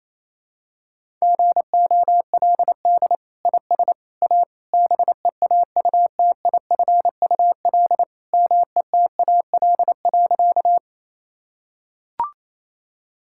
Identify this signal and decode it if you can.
Morse code, heard as a keyed tone, about 28 wpm, 700 Hz. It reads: GOLD IS A BEAUTIFUL METAL.
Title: Sentences from Top 1000 Words - Review Part 6 - 28wpm